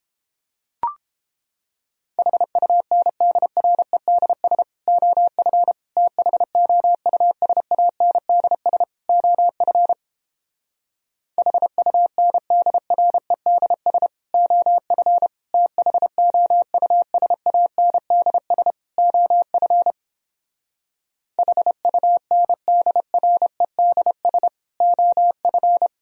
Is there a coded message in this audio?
Yes: Morse code from 2.2 to 25.9 s, heard as a keyed tone, about 31 wpm, 700 Hz. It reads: HUNDREDS OF THOUSANDS OF HUNDREDS OF THOUSANDS OF HUNDREDS OF